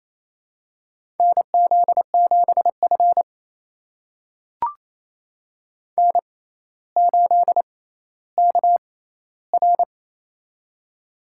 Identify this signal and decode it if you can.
Morse code, heard as a keyed tone, about 28 wpm, spaced as for 14 wpm, 700 Hz. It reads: NZ7F N8KR
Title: USA Call Signs - 4 Characters - 6x Character Spacing - 28wpm